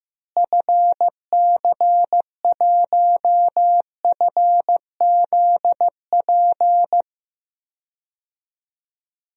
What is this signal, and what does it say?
Morse code, heard as a keyed tone, about 15 wpm, 700 Hz. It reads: FC1FZP
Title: DXCC-100 Call Signs - 6 Characters - Repeated 3x - 15wpm